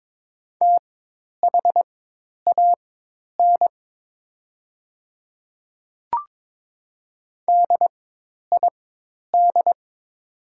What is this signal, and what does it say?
Morse code, heard as a keyed tone, about 22 wpm, spaced as for 10 wpm, 700 Hz. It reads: THAN DID
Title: 4x Character Spacing - Top 300 Words - 22wpm